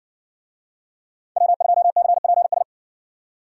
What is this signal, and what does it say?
Morse code, heard as a keyed tone, about 60 wpm, 700 Hz. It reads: W2CPS